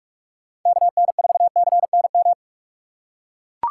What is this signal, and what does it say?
Morse code, heard as a keyed tone, about 43 wpm, 700 Hz. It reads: KN4CNM
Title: USA Call Signs - 4x Character Spacing - 45wpm